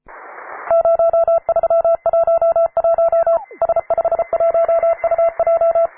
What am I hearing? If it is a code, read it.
Morse code, heard as a keyed tone, about 33 wpm, 670 Hz. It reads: 0311 S51UJ
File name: wae / cw advanced / p3f cw